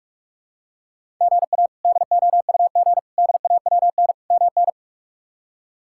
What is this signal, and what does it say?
Morse code, heard as a keyed tone, about 45 wpm, 700 Hz. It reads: GA DOUG DAWN MN
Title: K1USN SST QSO Practice - 45wpm